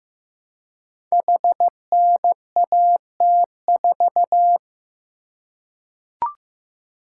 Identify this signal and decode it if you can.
Morse code, heard as a keyed tone, about 15 wpm, 700 Hz. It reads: HNAT4
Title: Code Groups - 5 Letter:Numbers - 15wpm